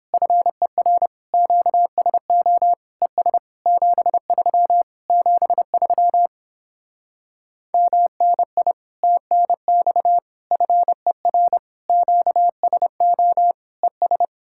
Morse code, heard as a keyed tone, about 29 wpm, 700 Hz. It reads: FER QSO ES 73 73 MNI TNX FER QSO ES